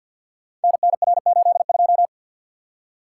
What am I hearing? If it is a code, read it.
Morse code, heard as a keyed tone, about 48 wpm, 700 Hz. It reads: NNR82
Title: Code Groups - 5 Letter:Numbers - 50wpm